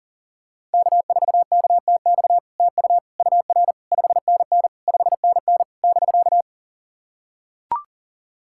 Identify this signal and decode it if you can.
Morse code, heard as a keyed tone, about 44 wpm, 700 Hz. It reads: K4KTX TU UR 5NN 5NN <BK>
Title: SOTA QSO Practice 40wpm